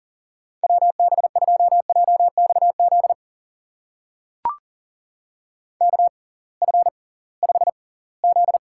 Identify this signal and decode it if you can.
Morse code, heard as a keyed tone, about 40 wpm, spaced as for 23 wpm, 700 Hz. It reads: WB2JXZ KF5Z
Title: USA Call Signs - 6x Character Spacing - 40wpm